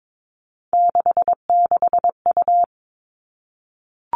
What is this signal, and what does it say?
Morse code, heard as a keyed tone, about 22 wpm, 700 Hz. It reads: 66U
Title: Three Challenging Letter-Number Combinations 22wpm